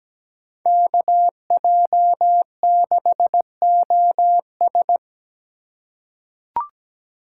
Morse code, heard as a keyed tone, about 17 wpm, 700 Hz. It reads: KJ6OS
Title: USA Call Signs - 4x Character Spacing - 17wpm